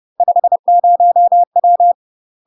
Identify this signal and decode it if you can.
Morse code, heard as a keyed tone, about 30 wpm, 700 Hz. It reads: 50W
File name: UK MASTER QSOs TRAINING / 30WPM / UK MASTER QSOs 0005